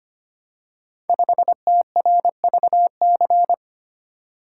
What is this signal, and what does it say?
Morse code, heard as a keyed tone, about 25 wpm, 690 Hz. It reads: 5TRVC
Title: Code Groups - 5 Letter:Numbers - 25wpm